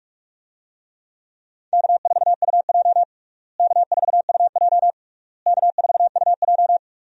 Morse code, heard as a keyed tone, about 46 wpm, 700 Hz. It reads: K4UJ K4UJ K4UJ
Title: USA Call Signs - 4 Characters - Repeated 3x - 45wpm